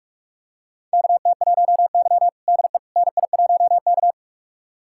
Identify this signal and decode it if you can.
Morse code, heard as a keyed tone, about 44 wpm, 690 Hz. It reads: KT1Y DE NI1K
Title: USA Callsign DE Callsign - 4 Characters - 45wpm